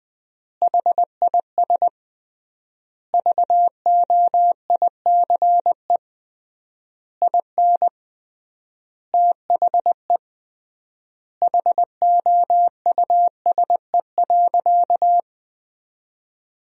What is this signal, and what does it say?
Morse code, heard as a keyed tone, about 20 wpm, 700 Hz. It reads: HIS VOICE IN THE HOUSE.
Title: 3x Word-Spacing - Sentences from Top 500 Words - 20wpm